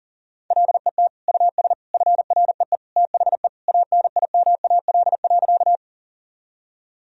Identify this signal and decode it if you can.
Morse code, heard as a keyed tone, about 40 wpm, 700 Hz. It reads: LET US FREE THE ANIMAL.
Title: Sentences from Top 1000 Words - Review Part 1 - 40wpm